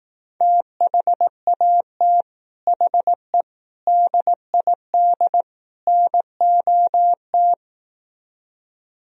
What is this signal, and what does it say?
Morse code, heard as a keyed tone, about 18 wpm, 700 Hz. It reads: THAT HE DID NOT
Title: Speed-Racing - Sets of 4 Words 15wpm